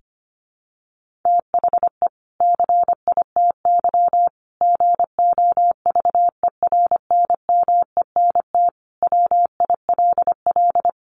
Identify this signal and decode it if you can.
Morse code, heard as a keyed tone, about 25 wpm, 700 Hz. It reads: THE CITY GOVERNMENT WILL
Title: Sentences from Top 1500 Words 25wpm